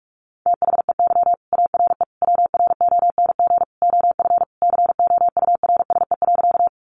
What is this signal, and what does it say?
Morse code, heard as a keyed tone, about 44 wpm, 700 Hz. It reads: THEY ARE WRONG OF COURSE.
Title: Sentences from Top 900 Words 45wpm